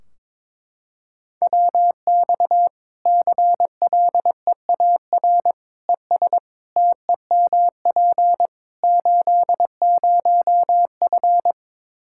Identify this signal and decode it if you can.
Morse code, heard as a keyed tone, about 22 wpm, 700 Hz. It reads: WX CLEAR ES TEMP 80F
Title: K1USN SST QSO Practice 22wpm